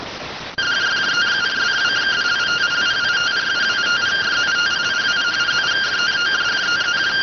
Two-tone teleprinter signal, about 170 Hz shift.